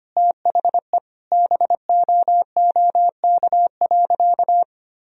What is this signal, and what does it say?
Morse code, heard as a keyed tone, about 25 wpm, 700 Hz. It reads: THE BOOK.
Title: Sentences from Top 1000 Words - Review Part 2 - 25wpm